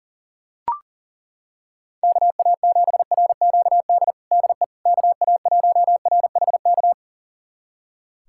Morse code, heard as a keyed tone, about 40 wpm, 700 Hz. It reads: KA7RQD DE KA1RHK
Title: USA Callsign DE Callsign - 6 Characters - 40wpm